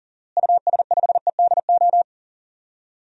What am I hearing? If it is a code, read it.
Morse code, heard as a keyed tone, about 40 wpm, 700 Hz. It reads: US5EDO
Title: DXCC-100 Call Signs - 6 Characters - Repeated 2x - 40wpm